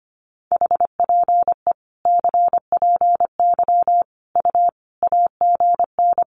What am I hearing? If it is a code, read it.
Morse code, heard as a keyed tone, about 25 wpm, 700 Hz. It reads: HPE CPY U AGN
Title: Common QSO Phrases 25wpm